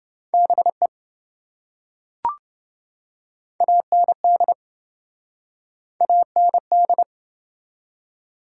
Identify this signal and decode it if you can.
Morse code, heard as a keyed tone, about 27 wpm, 700 Hz. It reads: BE AND AND